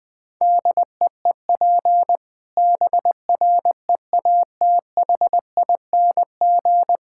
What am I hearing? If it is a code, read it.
Morse code, heard as a keyed tone, about 20 wpm, 700 Hz. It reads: DEEP BREATHING
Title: Difficult Sentences with Hints - 20wpm